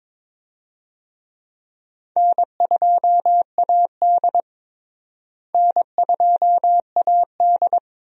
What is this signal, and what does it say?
Morse code, heard as a keyed tone, about 22 wpm, 700 Hz. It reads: N2AD N2AD
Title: USA Call Signs - 4 Characters - Repeated 3x - 22wpm